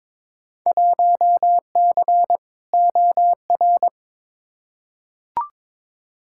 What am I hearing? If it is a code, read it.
Morse code, heard as a keyed tone, about 22 wpm, 700 Hz. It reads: 1C OR
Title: Field Day Exchanges - 22wpm